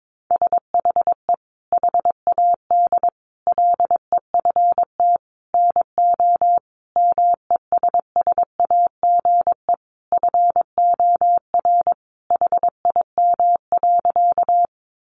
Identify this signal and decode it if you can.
Morse code, heard as a keyed tone, about 22 wpm, 700 Hz. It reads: SHE HAD LEFT NO MESSAGE FOR HIM.
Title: Sentences from Top 1100 Words 22wpm